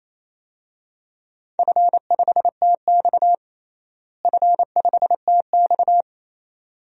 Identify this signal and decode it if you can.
Morse code, heard as a keyed tone, about 28 wpm, 700 Hz. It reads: F5TX F5TX
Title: DXCC-100 Call Signs - 4 Characters - Repeated 2x - 28wpm